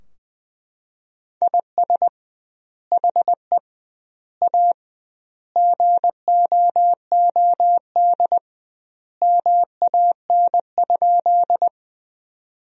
Morse code, heard as a keyed tone, about 20 wpm, 700 Hz. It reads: IS HE A GOOD MAN?
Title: 2x Word-Spacing - Sentences from Top 200 Words - 20wpm